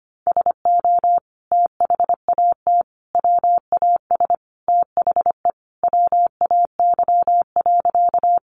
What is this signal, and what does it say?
Morse code, heard as a keyed tone, about 25 wpm, 700 Hz. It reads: SO THAT WAS THE WAY.
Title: Sentences from Top 100 Words 25wpm